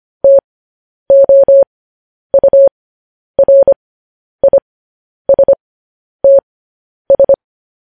Morse code, heard as a keyed tone, about 25 wpm, spaced as for 10 wpm, 550 Hz. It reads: TOURISTS